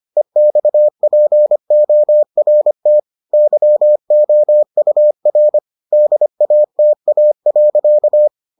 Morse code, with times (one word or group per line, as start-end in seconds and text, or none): 0.2-3.0 EXPORT
3.3-5.6 YOUR
5.9-8.3 DATA.